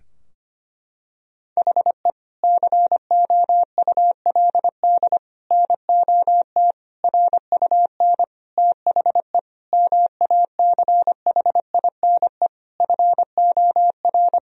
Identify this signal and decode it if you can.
Morse code, heard as a keyed tone, about 25 wpm, 700 Hz. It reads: HE COULD NOT RUN THE MACHINE FOR
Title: Sentences from Top 1000 Words - Review Part 2 - 25wpm